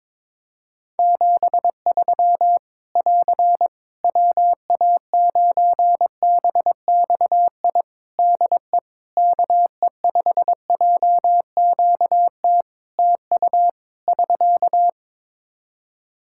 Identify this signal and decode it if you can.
Morse code, heard as a keyed tone, about 21 wpm, 700 Hz. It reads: 73 <AR> WA9BXI DE KE5JQT TU <SK>
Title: Practice QSO - Exchange 4 - Ending - 22wpm